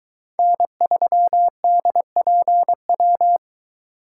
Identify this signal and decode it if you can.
Morse code, heard as a keyed tone, about 22 wpm, 700 Hz. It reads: N3DPW